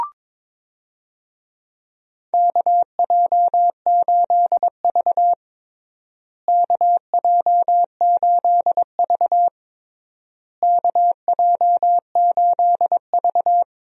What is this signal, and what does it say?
Morse code, heard as a keyed tone, about 22 wpm, 700 Hz. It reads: KJ8V KJ8V KJ8V
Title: USA Call Signs - 4 Characters - Repeated 3x - 22wpm